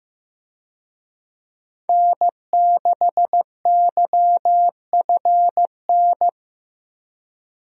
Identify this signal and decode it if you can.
Morse code, heard as a keyed tone, about 15 wpm, 700 Hz. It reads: N6YFN
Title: USA Call Signs - 5 Characters - Repeated 3x - 15wpm